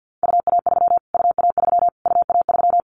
Morse code, heard as a keyed tone, about 50 wpm, 700 Hz. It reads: UA3 UA3 UA3
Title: Callsign Prefixes for Top 25 DXCC - Repeated 3x - 50wpm